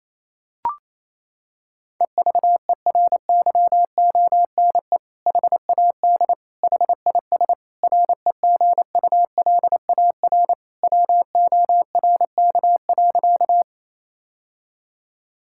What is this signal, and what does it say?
Morse code, heard as a keyed tone, about 28 wpm, 700 Hz. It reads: EVERYONE HAD HIS REGULAR WORK.
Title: Sentences from Top 1000 Words - Review Part 6 - 28wpm